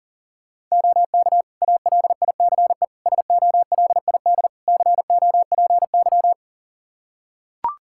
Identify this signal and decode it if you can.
Morse code, heard as a keyed tone, about 40 wpm, 700 Hz. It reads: OK ALICE SOLID COPY
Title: Practice QSO - Exchange 3 - More Info - 40wpm